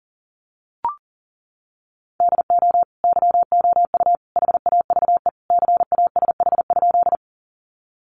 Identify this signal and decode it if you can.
Morse code, heard as a keyed tone, about 40 wpm, 700 Hz. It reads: DO YOU HAVE CASH?
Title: Sentences from Top 700 Words 40wpm